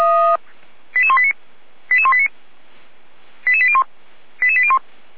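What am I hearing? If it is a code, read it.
Five-tone selective call (ZVEI), heard as quick runs of five tones, tones of about 70 ms. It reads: 88189 88189 88801 88801
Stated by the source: ZVEI five-tone selective call received with an SDR.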